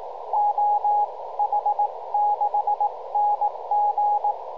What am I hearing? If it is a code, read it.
Morse code, heard as a keyed tone, about 18 wpm, spaced as for 16 wpm, 830 Hz, with radio noise. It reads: OH6NG